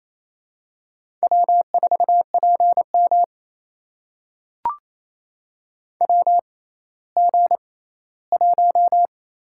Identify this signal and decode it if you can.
Morse code, heard as a keyed tone, about 28 wpm, spaced as for 16 wpm, 700 Hz. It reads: W4PM WG1
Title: USA Call Signs - 4 Characters - 6x Character Spacing - 28wpm